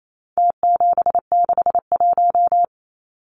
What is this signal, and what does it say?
Morse code, heard as a keyed tone, about 28 wpm, 700 Hz. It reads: T761
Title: Numbers - Contest Style - 4-Digit 28wpm